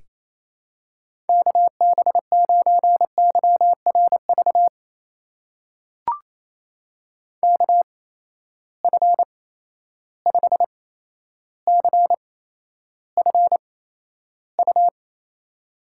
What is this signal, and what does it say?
Morse code, heard as a keyed tone, about 28 wpm, spaced as for 12 wpm, 700 Hz. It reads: KB9YRV KF5CFU
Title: USA Call Signs - 6 Characters - 8x Character Spacing - 28wpm